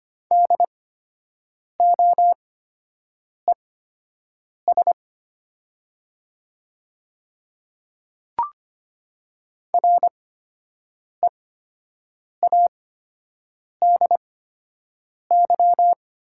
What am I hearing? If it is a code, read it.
Morse code, heard as a keyed tone, about 25 wpm, spaced as for 7 wpm, 700 Hz. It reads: DOES READY
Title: 8x Character Spacing - Top 500 Words - 25wpm